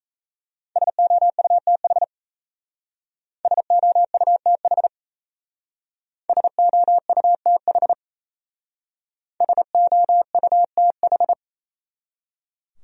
Morse code, heard as a keyed tone, about 35 wpm, 700 Hz. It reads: SOUTH SOUTH SOUTH SOUTH
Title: Speed-Racing Top 500 Words 28wpm